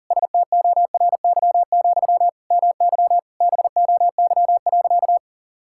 Morse code, heard as a keyed tone, about 40 wpm, 700 Hz. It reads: STORY, MY BOY.
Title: Sentences from Top 1000 Words - Review Part 6 - 40wpm